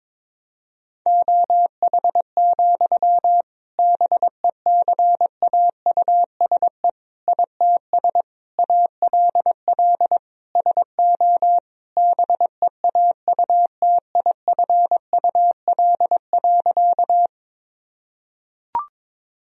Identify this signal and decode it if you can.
Morse code, heard as a keyed tone, about 22 wpm, 700 Hz. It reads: OH, BECAUSE ITS ALL SO BEAUTIFUL.
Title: Sentences from Top 1000 Words - Review Part 3 - 22wpm